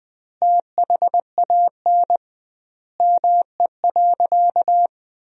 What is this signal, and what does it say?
Morse code, heard as a keyed tone, about 20 wpm, 700 Hz. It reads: THAN ME.
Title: 2x Word-Spacing - Sentences from Top 400 Words - 20wpm